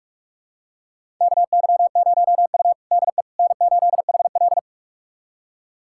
Morse code, heard as a keyed tone, about 45 wpm, 700 Hz. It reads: KY0U DE N8HL
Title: USA Callsign DE Callsign - 4 Characters - 45wpm